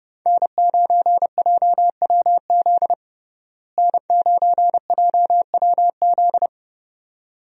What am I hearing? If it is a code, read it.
Morse code, heard as a keyed tone, about 30 wpm, 700 Hz. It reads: N9JWZ N9JWZ
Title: USA Call Signs - 5 Characters - Repeated 2x - 30wpm